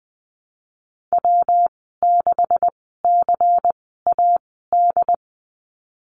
Morse code, heard as a keyed tone, about 20 wpm, spaced as for 14 wpm, 700 Hz. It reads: W6CAD
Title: 140 Lesson 23g - Letter f - Call Signs 20wpm